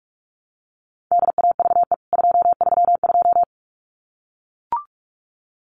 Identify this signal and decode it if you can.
Morse code, heard as a keyed tone, about 44 wpm, 700 Hz. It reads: DAVE 232
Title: CWT Exchange 45wpm